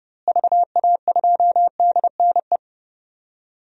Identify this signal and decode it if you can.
Morse code, heard as a keyed tone, about 31 wpm, 700 Hz. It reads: VA2DNE
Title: DXCC-100 Call Signs - 6 Characters - Repeated 2x - 30wpm